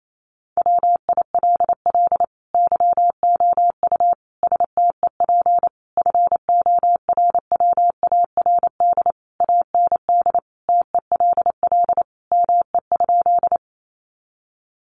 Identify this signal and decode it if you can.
Morse code, heard as a keyed tone, about 28 wpm, 700 Hz. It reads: WILL YOU STEP FORWARD AND TELL ME?
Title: Sentences from Top 1300 Words 28wpm